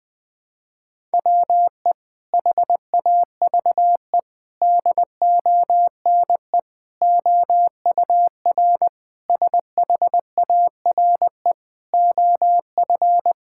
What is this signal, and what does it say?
Morse code, heard as a keyed tone, about 20 wpm, 700 Hz. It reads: WE HAVE DONE OUR SHARE OF
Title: Sentences from Top 1000 Words - Review Part 6 - 20wpm